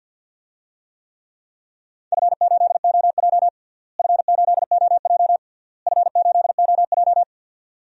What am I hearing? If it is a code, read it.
Morse code, heard as a keyed tone, about 48 wpm, 700 Hz. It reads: F8OJ F8OJ F8OJ
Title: DXCC-100 Call Signs - 4 Characters - Repeated 3x - 50wpm